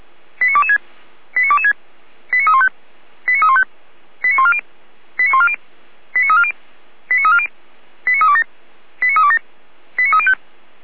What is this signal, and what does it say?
Five-tone selective call (ZVEI), heard as quick runs of five tones, tones of about 70 ms. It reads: 89226 89226 89315 89315 89140 89140 89340 89340 89327 89327 89305